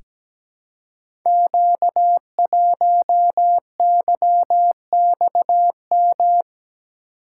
Morse code, heard as a keyed tone, about 17 wpm, 700 Hz. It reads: Q1YXM
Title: Code Groups - 5 Letter:Numbers - 17wpm